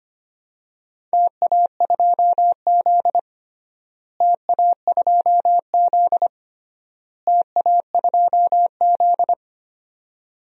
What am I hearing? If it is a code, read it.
Morse code, heard as a keyed tone, about 25 wpm, 700 Hz. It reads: TA2Z TA2Z TA2Z
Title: DXCC-100 Call Signs - 4 Characters - Repeated 3x - 25wpm